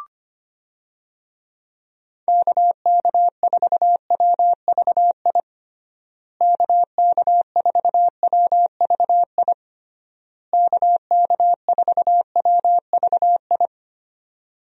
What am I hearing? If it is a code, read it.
Morse code, heard as a keyed tone, about 25 wpm, 700 Hz. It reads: KK4WVI KK4WVI KK4WVI
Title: USA Call Signs - 6 Characters - Repeated 3x - 25wpm